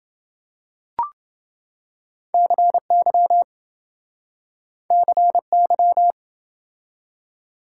Morse code, heard as a keyed tone, about 28 wpm, 700 Hz. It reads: CY CY